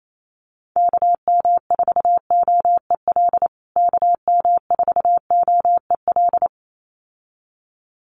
Morse code, heard as a keyed tone, about 28 wpm, 700 Hz. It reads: KM4OEL KM4OEL
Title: USA Call Signs Repeated 2x 28wpm